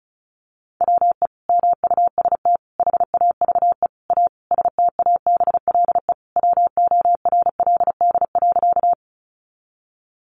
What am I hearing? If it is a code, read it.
Morse code, heard as a keyed tone, about 34 wpm, 700 Hz. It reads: WE MUST HAVE A STABLE WORLD.
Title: Sentences from Top 1500 Words 35wpm